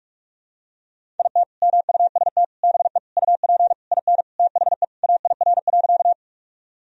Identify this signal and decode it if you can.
Morse code, heard as a keyed tone, about 44 wpm, 700 Hz. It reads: IT MUST BE UP IN THE AIR.
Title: Sentences from Top 1000 Words - Review Part 1 - 45wpm